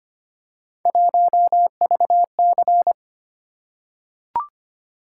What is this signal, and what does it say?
Morse code, heard as a keyed tone, about 25 wpm, 700 Hz. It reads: 1VC
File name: Three Letter-Number Combinations - 25wpm